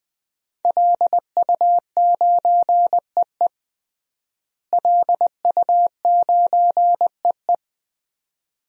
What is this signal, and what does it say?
Morse code, heard as a keyed tone, about 20 wpm, 700 Hz. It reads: LU9EE LU9EE
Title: DXCC-100 Call Signs - 5 Characters - Repeated 2x - 20wpm